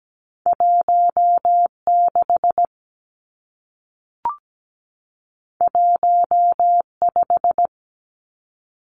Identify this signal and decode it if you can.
Morse code, heard as a keyed tone, about 17 wpm, 700 Hz. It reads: 16 15
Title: Two Challenging Letter-Number Combinations 17wpm